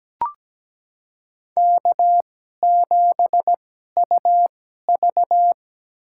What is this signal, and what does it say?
Morse code, heard as a keyed tone, about 17 wpm, spaced as for 12 wpm, 700 Hz. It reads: K7UV
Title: USA Call Signs - 2x Character Spacing - 17wpm